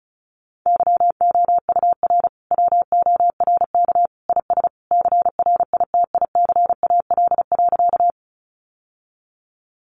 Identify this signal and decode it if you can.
Morse code, heard as a keyed tone, about 34 wpm, 700 Hz. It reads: YOUR WORK IS CRITICAL.
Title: Sentences from Top 1100 Words 35wpm